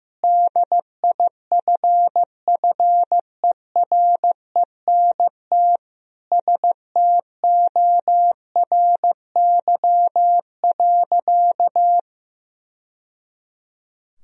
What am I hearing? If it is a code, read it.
Morse code, heard as a keyed tone, about 15 wpm, 700 Hz. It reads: DIFFERENT STORY.
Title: Sentences from Top 1000 Words - Review Part 3 - 15wpm